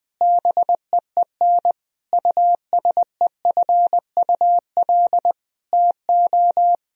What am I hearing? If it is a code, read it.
Morse code, heard as a keyed tone, about 19 wpm, 700 Hz. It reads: BEEN USEFUL TO
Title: Sentences from Top 1000 Words - Review Part 4 - 20wpm